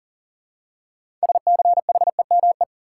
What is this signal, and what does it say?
Morse code, heard as a keyed tone, about 40 wpm, 700 Hz. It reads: SCHEME
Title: Six Letter Words - 40wpm